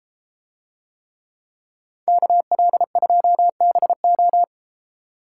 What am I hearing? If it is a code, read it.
Morse code, heard as a keyed tone, about 33 wpm, 700 Hz. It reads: KL2BO